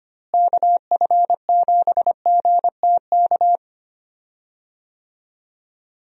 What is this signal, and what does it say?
Morse code, heard as a keyed tone, about 25 wpm, 700 Hz. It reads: KF7GTK
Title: USA Call Signs - 6 Characters - Repeated 2x - 25wpm